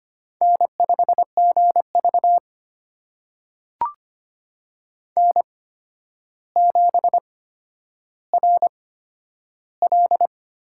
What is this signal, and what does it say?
Morse code, heard as a keyed tone, about 25 wpm, spaced as for 11 wpm, 700 Hz. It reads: N5GV N7RL
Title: USA Call Signs - 4 Characters - 8x Character Spacing - 25wpm